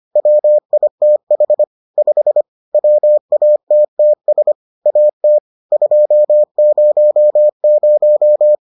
Morse code, heard as a keyed tone, about 25 wpm, 600 Hz. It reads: WITH 5 WATTS AT 200